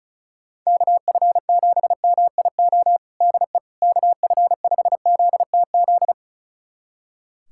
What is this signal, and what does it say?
Morse code, heard as a keyed tone, about 36 wpm, 700 Hz. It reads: KF7MIO DE KF5ZTZ
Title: USA Callsign DE Callsign 35wpm